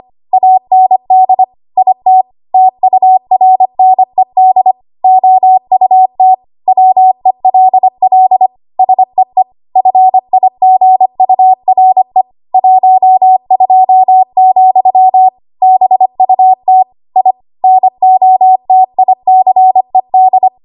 Morse code, heard as a keyed tone, about 25 wpm, 750 Hz. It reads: AND IT TURNED OUT WELL SEE FIGURE 12, BUT I NOTICED
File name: Archive / 25 WPM / 25WPM